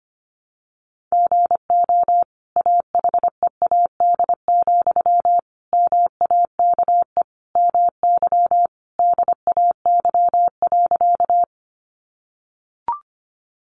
Morse code, heard as a keyed tone, about 25 wpm, 700 Hz. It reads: GO AHEAD, MAKE MY DAY.
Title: Sentences from Top 1100 Words 25wpm